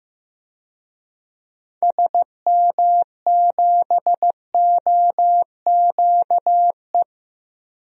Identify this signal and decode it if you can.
Morse code, heard as a keyed tone, about 15 wpm, 700 Hz. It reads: SM7OQE
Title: DXCC-100 Call Signs - 6 Characters - 15wpm